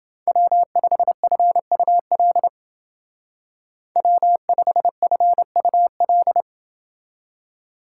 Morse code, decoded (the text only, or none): W5FUL W5FUL